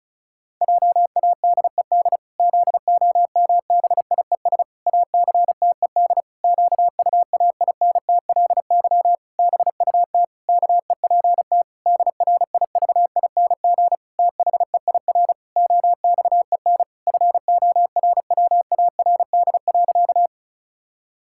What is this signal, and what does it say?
Morse code, heard as a keyed tone, about 36 wpm, 700 Hz. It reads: JADED ZOMBIES ACTED QUAINTLY BUT KEPT DRIVING THEIR OXEN FORWARD.
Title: Pangram Challenge v2 - 35wpm